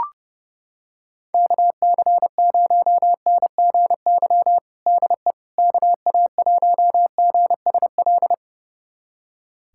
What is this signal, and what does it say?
Morse code, heard as a keyed tone, about 30 wpm, 700 Hz. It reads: KC0NGY DE KA1GSL
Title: USA Callsign DE Callsign - 6 Characters - 30wpm